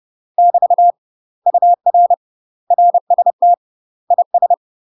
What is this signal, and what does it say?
Morse code, heard as a keyed tone, about 30 wpm, 700 Hz. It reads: <BT> UR RST IS
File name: UK MASTER QSOs TRAINING / 30WPM / UK MASTER QSOs 0011